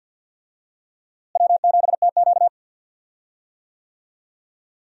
Morse code, heard as a keyed tone, about 51 wpm, 700 Hz. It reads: W7TQ
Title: USA Call Signs - 4 Characters - Repeated 3x - 50wpm